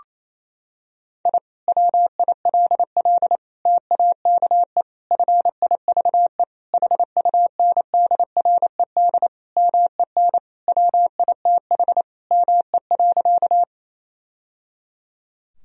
Morse code, heard as a keyed tone, about 27 wpm, 700 Hz. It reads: I WILL TAKE FIVE HUNDRED MEN WITH ME.
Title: Sentences from Top 1000 Words - Review Part 2 - 28wpm